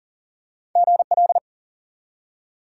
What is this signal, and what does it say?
Morse code, heard as a keyed tone, about 41 wpm, 700 Hz. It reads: GL
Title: Top 180 QSO Elements 40wpm